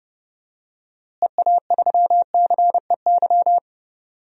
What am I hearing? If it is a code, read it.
Morse code, heard as a keyed tone, about 30 wpm, 700 Hz. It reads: EA3CEY
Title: DXCC-100 Call Signs - 6 Characters - Repeated 2x - 30wpm